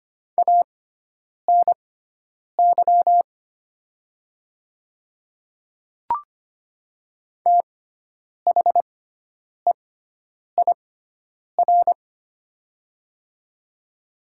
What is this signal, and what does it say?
Morse code, heard as a keyed tone, about 25 wpm, spaced as for 9 wpm, 700 Hz. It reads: ANY THEIR
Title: 6x Character Spacing - Top 500 Words - 25wpm